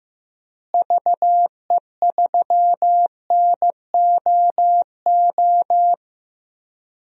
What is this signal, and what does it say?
Morse code, heard as a keyed tone, about 15 wpm, 700 Hz. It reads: VE3NOO
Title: DXCC-100 Call Signs - 6 Characters - Repeated 2x - 15wpm